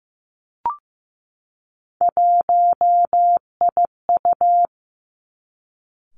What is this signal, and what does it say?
Morse code, heard as a keyed tone, about 15 wpm, 700 Hz. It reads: 1IU
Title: Three Challenging Letter-Number Combinations 15wpm